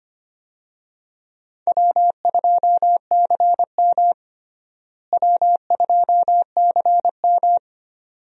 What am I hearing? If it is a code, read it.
Morse code, heard as a keyed tone, about 25 wpm, 700 Hz. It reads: W2CM W2CM
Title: USA Call Signs - 4 Characters - Repeated 2x - 25wpm